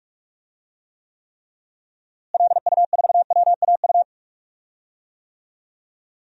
Morse code, heard as a keyed tone, about 41 wpm, 700 Hz. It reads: LU4WAU
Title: DXCC-100 Call Signs - 6 Characters - Repeated 3x - 45wpm